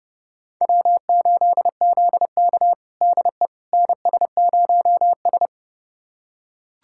Morse code, heard as a keyed tone, about 32 wpm, 700 Hz. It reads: W8ZK DE NS0S
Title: USA Callsign DE Callsign - 4 Characters - 30wpm